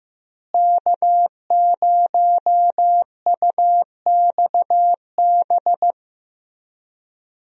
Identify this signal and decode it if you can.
Morse code, heard as a keyed tone, about 15 wpm, 700 Hz. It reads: K0UXB
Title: USA Call Signs - 5 Characters - Repeated 2x - 15wpm